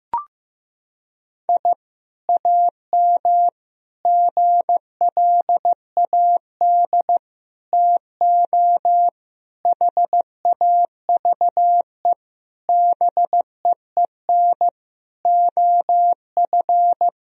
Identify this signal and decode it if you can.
Morse code, heard as a keyed tone, about 16 wpm, 700 Hz. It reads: I AM GLAD TO HAVE BEEN OF
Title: Sentences from Top 1000 Words - Review Part 4 - 15wpm